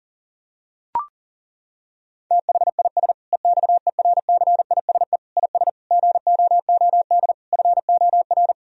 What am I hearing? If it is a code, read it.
Morse code, heard as a keyed tone, about 40 wpm, 700 Hz. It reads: THIS EXERCISE IS GOOD FOR
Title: Sentences from Top 1000 Words - Review Part 4 - 40wpm